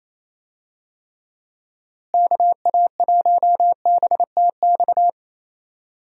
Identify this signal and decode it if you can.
Morse code, heard as a keyed tone, about 28 wpm, 700 Hz. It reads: KA1BTX
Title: USA Call Signs - 6 Characters - Repeated 2x - 28wpm